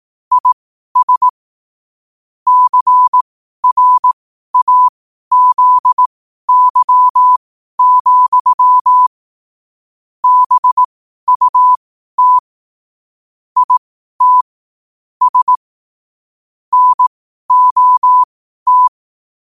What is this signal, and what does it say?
Morse code, heard as a keyed tone, about 18 wpm, spaced as for 12 wpm, 1000 Hz. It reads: IS CRAZY, BUT IT S NOT
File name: qotdfiles / 12wpm